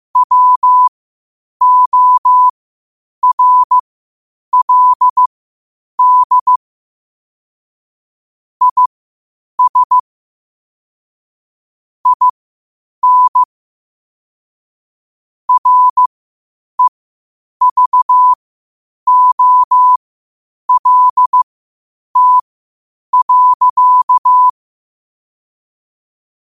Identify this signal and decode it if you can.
Morse code, heard as a keyed tone, about 15 wpm, spaced as for 8 wpm, 1000 Hz. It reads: WORLD IS IN REVOLT.
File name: qotdfiles / 8wpm